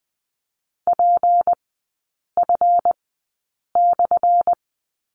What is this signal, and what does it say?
Morse code, heard as a keyed tone, about 20 wpm, spaced as for 8 wpm, 700 Hz. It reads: PF/
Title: 178 Lesson 29e - Slash - Three Characters 20wpm